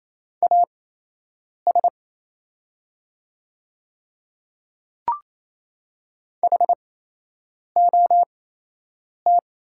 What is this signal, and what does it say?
Morse code, heard as a keyed tone, about 27 wpm, spaced as for 8 wpm, 700 Hz. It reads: AS HOT
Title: 8x Character Spacing - Top 200 Words - 28wpm